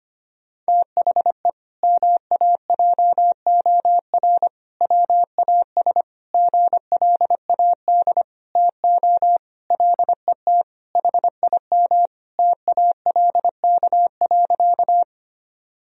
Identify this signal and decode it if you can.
Morse code, heard as a keyed tone, about 25 wpm, 700 Hz. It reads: THE MAJOR WAS GLAD TO LET HIM TALK.
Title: Sentences from Top 1000 Words - Review Part 4 - 25wpm